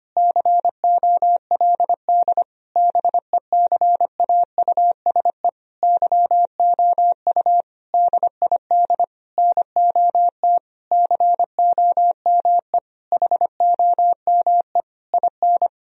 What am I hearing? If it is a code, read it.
Morse code, heard as a keyed tone, about 25 wpm, 700 Hz. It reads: COLD BECAUSE YOU DID NOT COME HOME IN
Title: Sentences from Top 1000 Words - Review Part 3 - 25wpm